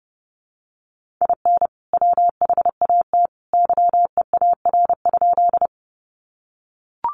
Morse code, heard as a keyed tone, about 30 wpm, 700 Hz. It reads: IN WHAT YEAR?
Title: Sentences from Top 200 Words 30wpm